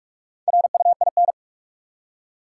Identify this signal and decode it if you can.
Morse code, heard as a keyed tone, about 42 wpm, 700 Hz. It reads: RUIN
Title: Four Letter Words - 45wpm